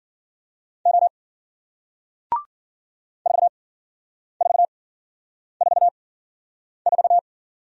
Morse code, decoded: O 4444

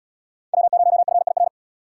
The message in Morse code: W0FIR

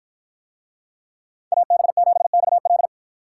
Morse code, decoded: AB7XL

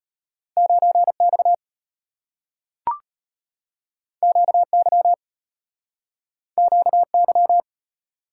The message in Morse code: 9X QY QY

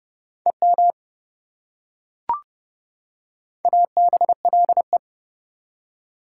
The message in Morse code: EM ABLE

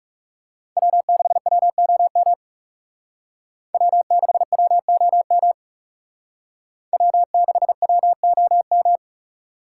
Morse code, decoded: W6WOM W6WOM W6WOM

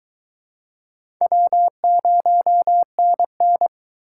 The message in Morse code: W0NN

